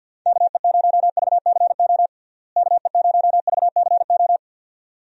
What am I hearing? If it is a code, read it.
Morse code, heard as a keyed tone, about 48 wpm, 700 Hz. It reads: KE0VCO KE0VCO
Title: USA Call Signs - 6 Characters - Repeated 2x - 50wpm